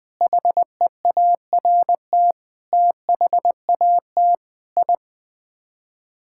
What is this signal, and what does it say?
Morse code, heard as a keyed tone, about 20 wpm, 700 Hz. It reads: HEART THAT I
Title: Sets of 8 Words - 20wpm